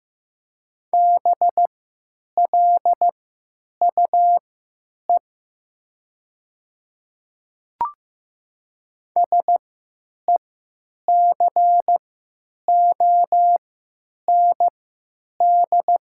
Morse code, heard as a keyed tone, about 15 wpm, spaced as for 9 wpm, 700 Hz. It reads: BLUE SECOND
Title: 3x Character Spacing - Top 500 Words - 15wpm